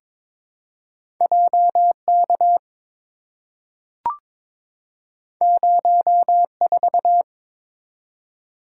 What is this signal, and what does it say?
Morse code, heard as a keyed tone, about 22 wpm, 700 Hz. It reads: JK 04
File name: Two Letter-Number Combinations - 22wpm